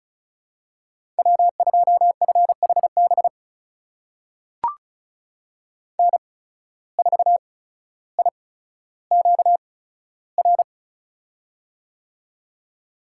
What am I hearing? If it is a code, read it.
Morse code, heard as a keyed tone, about 36 wpm, spaced as for 15 wpm, 700 Hz. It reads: W2FHB N4IQR
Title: USA Call Signs - 5 Characters - 8x Character Spacing - 35wpm